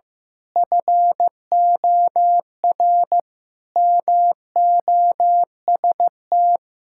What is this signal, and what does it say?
Morse code, heard as a keyed tone, about 15 wpm, 700 Hz. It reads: FOR MOST
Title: Pangram Challenge v2 - 15wpm